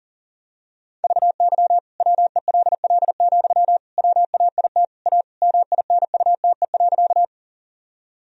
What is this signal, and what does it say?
Morse code, heard as a keyed tone, about 40 wpm, 700 Hz. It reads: VY WELL, WAIT A MINUTE.
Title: Sentences from Top 500 Words and Abbreviations 40wpm